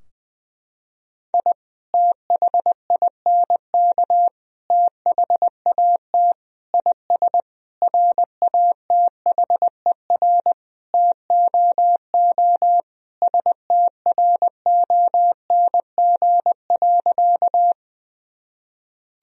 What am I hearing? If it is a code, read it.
Morse code, heard as a keyed tone, about 21 wpm, 700 Hz. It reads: I THINK THAT IS RATHER TOO STRONG.
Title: Sentences from Top 1000 Words - Review Part 4 - 20wpm